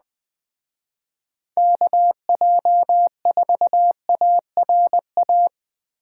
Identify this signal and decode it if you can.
Morse code, heard as a keyed tone, about 20 wpm, 700 Hz. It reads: KJ4ARA
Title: Speed Racing - USA 2x3 Call Signs 17wpm